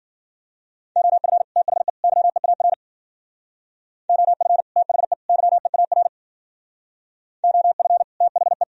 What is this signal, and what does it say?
Morse code, heard as a keyed tone, about 50 wpm, 700 Hz. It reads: OF THE YEAR OF THE YEAR OF THE